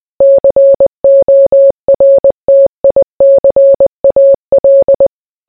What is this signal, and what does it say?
Morse code, heard as a keyed tone, about 20 wpm, 550 Hz. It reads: CORTICAL